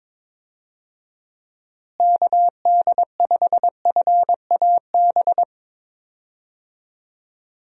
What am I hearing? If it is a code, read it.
Morse code, heard as a keyed tone, about 22 wpm, 700 Hz. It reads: KD5FAB
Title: USA Call Signs - 6 Characters - Repeated 2x - 22wpm